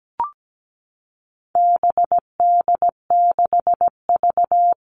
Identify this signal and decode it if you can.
Morse code, heard as a keyed tone, about 17 wpm, 700 Hz. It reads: BD6V
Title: All Cut Numbers - 4-Digit 17wpm